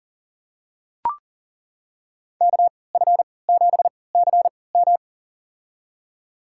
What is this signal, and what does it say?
Morse code, heard as a keyed tone, about 38 wpm, spaced as for 22 wpm, 700 Hz. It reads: KF7CM
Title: USA Call Signs - 5 Characters - 3x Character Spacing - 40wpm